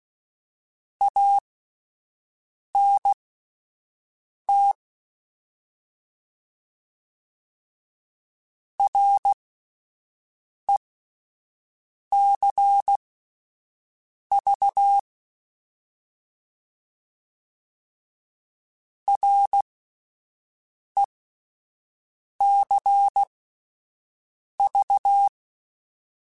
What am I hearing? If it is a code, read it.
Morse code, heard as a keyed tone, about 16 wpm, spaced as for 5 wpm, 800 Hz. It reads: ANT RECV RECV
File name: Abreviations / Abreviations mp3s / Abreviations@5